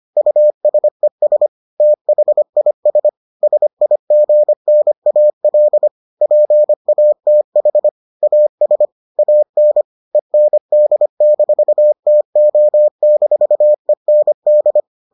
Morse code, heard as a keyed tone, about 25 wpm, 600 Hz. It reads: USES THIS SIGNAL PATH AS AN END-TO-END